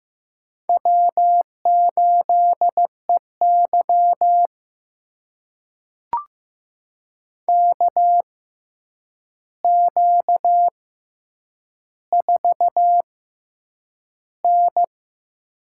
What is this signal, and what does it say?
Morse code, heard as a keyed tone, about 15 wpm, spaced as for 8 wpm, 700 Hz. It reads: W8EY KQ4N
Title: USA Call Signs - 4 Characters - 6x Character Spacing - 15wpm